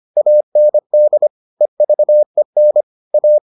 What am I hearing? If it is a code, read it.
Morse code, heard as a keyed tone, about 25 wpm, 600 Hz. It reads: AND EVEN A